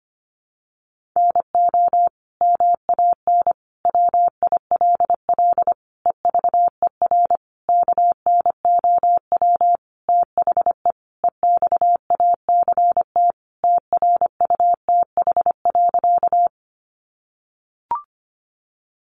Morse code, heard as a keyed tone, about 25 wpm, 700 Hz. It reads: NO MAN WILL EVER KNOW THE EXACT TRUTH.
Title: Sentences from Top 1300 Words 25wpm